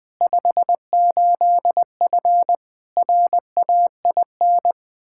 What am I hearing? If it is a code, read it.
Morse code, heard as a keyed tone, about 20 wpm, 700 Hz. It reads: 58F RAIN
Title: Practice QSO - Exchange 2 - Info - 20wpm